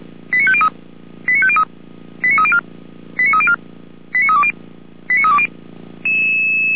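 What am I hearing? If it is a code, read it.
Five-tone selective call (ZVEI), heard as quick runs of five tones, tones of about 70 ms. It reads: 89503 89503 89304 89304 89320 89320
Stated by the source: ZVEI five-tone selective call received on SDR.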